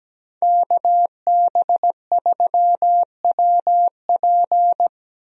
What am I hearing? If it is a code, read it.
Morse code, heard as a keyed tone, about 17 wpm, 700 Hz. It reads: KB3WP
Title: USA Call Signs - 5 Characters - Repeated 2x - 17wpm